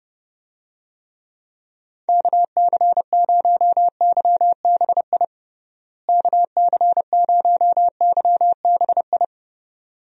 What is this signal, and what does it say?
Morse code, heard as a keyed tone, about 30 wpm, 700 Hz. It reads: KC0YBI KC0YBI
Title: USA Call Signs - 6 Characters - Repeated 2x - 30wpm